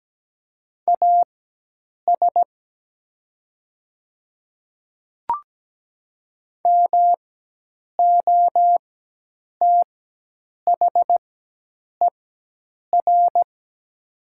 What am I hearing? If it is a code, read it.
Morse code, heard as a keyed tone, about 17 wpm, spaced as for 8 wpm, 700 Hz. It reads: AS MOTHER
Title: 4x Character Spacing - Top 200 Words - 17wpm